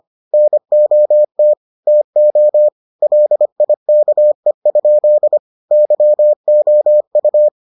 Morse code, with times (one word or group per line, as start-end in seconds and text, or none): 0.3-1.5 NOT
1.9-2.7 TO
3.0-5.4 LIKE?
5.7-7.5 YOU